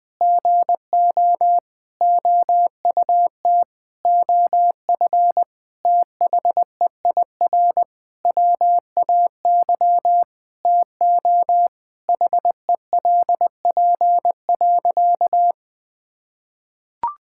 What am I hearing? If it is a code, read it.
Morse code, heard as a keyed tone, about 20 wpm, 700 Hz. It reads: GO OUT OF THEIR WAY TO HELP.
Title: Sentences with 6-Word Phrases - 20wpm